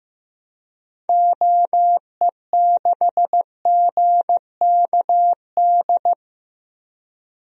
Morse code, heard as a keyed tone, about 15 wpm, 700 Hz. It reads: OE6GKD
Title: DXCC-100 Call Signs - 6 Characters - Repeated 2x - 15wpm